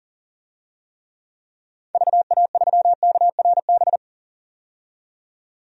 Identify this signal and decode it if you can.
Morse code, heard as a keyed tone, about 40 wpm, 700 Hz. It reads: VA3KRB
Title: DXCC-100 Call Signs - 6 Characters - 40wpm